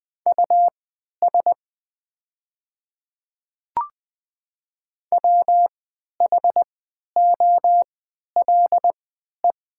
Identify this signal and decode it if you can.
Morse code, heard as a keyed tone, about 20 wpm, spaced as for 11 wpm, 700 Hz. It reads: US WHOLE
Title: 3x Character Spacing - Top 500 Words - 20wpm